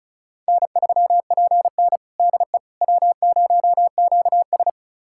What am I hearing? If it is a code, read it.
Morse code, heard as a keyed tone, about 34 wpm, 700 Hz. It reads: N3PN DE W0QS
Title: USA Callsign DE Callsign - 4 Characters - 35wpm